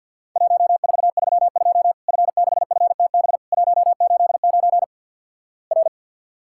Morse code, heard as a keyed tone, about 50 wpm, 700 Hz. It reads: 1432 F6FTB 189 R
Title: Worked All Europe DX Contest - QTC Exchange - 50wpm